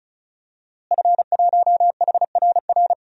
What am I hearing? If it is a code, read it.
Morse code, heard as a keyed tone, about 36 wpm, 700 Hz. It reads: F1HRR
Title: DXCC-100 Call Signs - 5 Characters - Repeated 3x - 35wpm